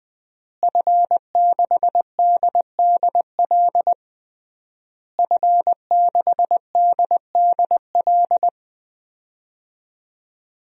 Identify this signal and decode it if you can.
Morse code, heard as a keyed tone, about 20 wpm, 700 Hz. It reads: F6DDL F6DDL
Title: DXCC-100 Call Signs - 5 Characters - Repeated 3x - 20wpm